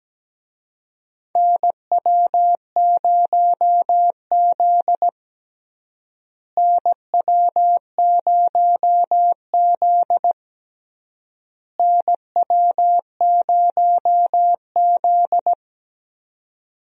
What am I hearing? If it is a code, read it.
Morse code, heard as a keyed tone, about 17 wpm, 700 Hz. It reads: NW0Z NW0Z NW0Z
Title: USA Call Signs - 4 Characters - Repeated 3x - 17wpm